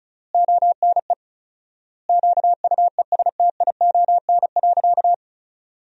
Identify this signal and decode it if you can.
Morse code, heard as a keyed tone, about 34 wpm, 700 Hz. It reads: ONE QUESTION.
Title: 4x Word-Spacing - Sentences from Top 400 Words - 35wpm